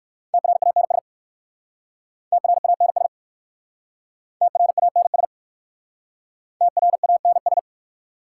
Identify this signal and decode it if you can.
Morse code, heard as a keyed tone, about 57 wpm, 690 Hz. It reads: TRANS TRANS TRANS TRANS